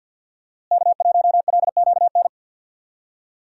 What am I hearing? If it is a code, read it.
Morse code, heard as a keyed tone, about 52 wpm, 700 Hz. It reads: K1LQN
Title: USA Call Signs - 5 Characters - Repeated 2x - 50wpm